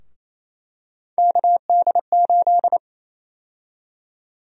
Morse code, heard as a keyed tone, about 28 wpm, 700 Hz. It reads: KD8
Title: USA Call Sign Prefixes - Repeated 2x - 28wpm